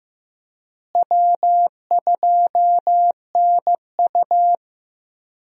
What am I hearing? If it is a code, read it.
Morse code, heard as a keyed tone, about 15 wpm, 700 Hz. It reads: W2NU
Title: USA Call Signs - 4 Characters - Repeated 3x - 15wpm